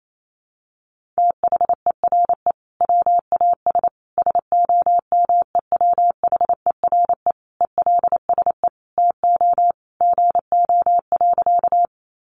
Morse code, heard as a keyed tone, about 28 wpm, 700 Hz. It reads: THERE WAS SOMEWHERE ELSE TO GO.
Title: Sentences from Top 1400 Words 28wpm